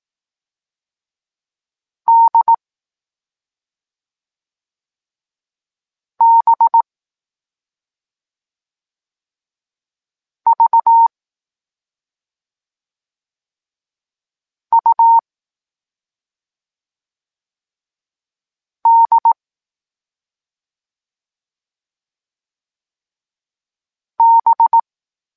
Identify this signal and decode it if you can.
Morse code, heard as a keyed tone, about 18 wpm, 920 Hz. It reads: D B V U D B
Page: Tekst 14 – doskonalenie odbioru poznanych liter